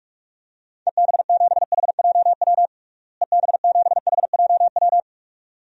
Morse code, heard as a keyed tone, about 45 wpm, 700 Hz. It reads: EB7HJW EB7HJW
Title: DXCC-100 Call Signs - 6 Characters - Repeated 2x - 45wpm